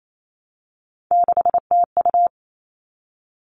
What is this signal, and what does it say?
Morse code, heard as a keyed tone, about 28 wpm, 700 Hz. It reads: 6TU
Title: All Cut Numbers - 3-Digit 28wpm